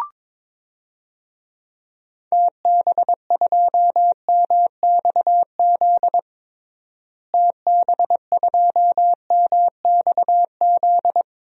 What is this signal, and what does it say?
Morse code, heard as a keyed tone, about 22 wpm, 700 Hz. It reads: TB2MXZ TB2MXZ
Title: DXCC-100 Call Signs - 6 Characters - Repeated 2x - 22wpm